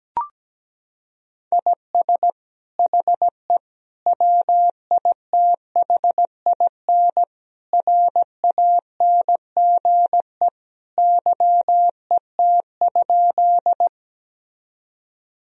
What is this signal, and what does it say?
Morse code, heard as a keyed tone, about 17 wpm, 700 Hz. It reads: IS HE WITHIN RANGE YET?
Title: Sentences from Top 1000 Words - Review Part 4 - 17wpm